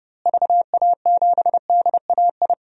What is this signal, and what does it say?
Morse code, heard as a keyed tone, about 32 wpm, 700 Hz. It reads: VA7DAI
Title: DXCC-100 Call Signs - 6 Characters - Repeated 3x - 30wpm